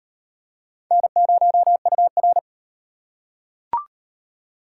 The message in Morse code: N0UR

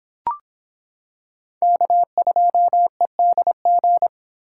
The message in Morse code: K2EDG

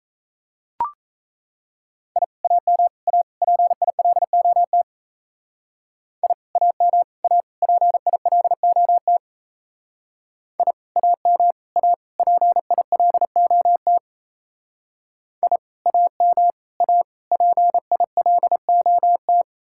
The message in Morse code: I AM A PILOT I AM A PILOT I AM A PILOT I AM A PILOT

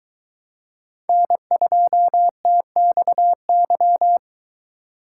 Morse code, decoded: N2TXY

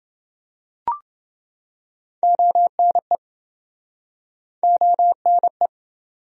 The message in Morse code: ONE ONE

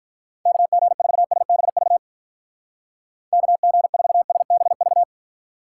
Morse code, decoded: KG4SBV KG4SBV

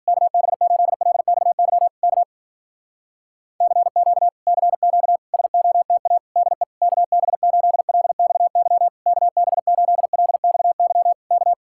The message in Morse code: KB8LXY K CQ CQ SOTA DE KB8LXY KB8LXY K

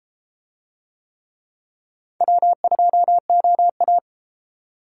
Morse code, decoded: W2OA

